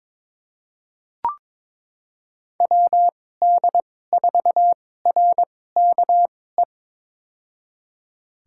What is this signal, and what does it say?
Morse code, heard as a keyed tone, about 21 wpm, spaced as for 16 wpm, 700 Hz. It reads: WD4RKE
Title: USA Call Signs - 6 Characters - 2x Character Spacing - 22wpm